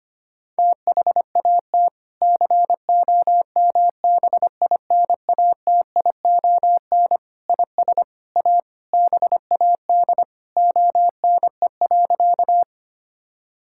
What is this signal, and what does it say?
Morse code, heard as a keyed tone, about 25 wpm, 700 Hz. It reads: THAT COMBINATION IS A BAD ONE.
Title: Sentences from Top 1000 Words - Review Part 6 - 25wpm